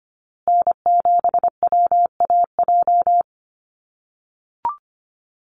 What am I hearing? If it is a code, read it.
Morse code, heard as a keyed tone, about 25 wpm, 700 Hz. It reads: N7WAJ
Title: USA Call Signs Repeated 2x 25wpm